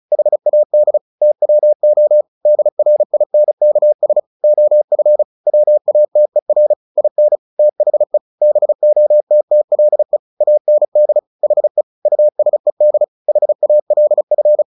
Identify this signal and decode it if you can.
Morse code, heard as a keyed tone, about 34 wpm, 600 Hz. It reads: HAD TWO DRINKS OF WATER IN THE BOTTLE AND HE USED HALF
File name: the old man and the sea 2 35wpm